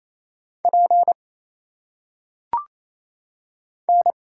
Morse code, heard as a keyed tone, about 27 wpm, 700 Hz. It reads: P N
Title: Single Letters - Phonetic Pronunciation 28wpm